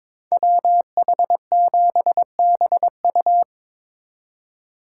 Morse code, heard as a keyed tone, about 25 wpm, 700 Hz. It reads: WH7BU
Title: Speed Racing - USA Call Signs 22wpm